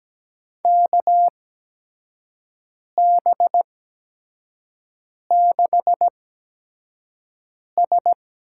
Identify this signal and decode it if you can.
Morse code, heard as a keyed tone, about 17 wpm, spaced as for 5 wpm, 700 Hz. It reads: KB6S
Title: USA Call Signs - 8x Character Spacing - 17wpm